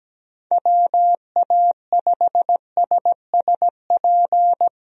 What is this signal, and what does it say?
Morse code, heard as a keyed tone, about 17 wpm, 700 Hz. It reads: WA5SSP
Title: USA Call Signs - 6 Characters - 17wpm